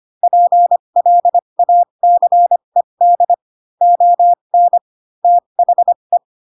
Morse code, heard as a keyed tone, about 25 wpm, spaced as for 22 wpm, 700 Hz. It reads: PLACED ON THE
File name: Pinocchio / Pinocchio ch7 18 25 ik1hge